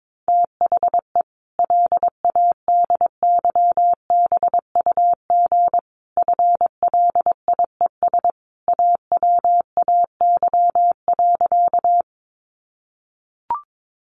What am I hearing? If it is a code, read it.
Morse code, heard as a keyed tone, about 22 wpm, 700 Hz. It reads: THE LADYBUG FLIES AWAY.
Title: Additional Easy Phrases 22wpm